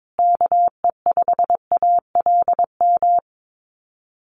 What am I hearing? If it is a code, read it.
Morse code, heard as a keyed tone, about 22 wpm, 700 Hz. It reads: KE5ALM
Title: USA Call Signs Repeated 3x 22wpm